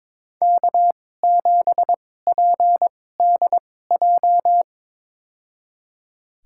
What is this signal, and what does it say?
Morse code, heard as a keyed tone, about 22 wpm, spaced as for 16 wpm, 700 Hz. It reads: K7PDJ
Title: USA Call Signs - 2x Character Spacing - 22wpm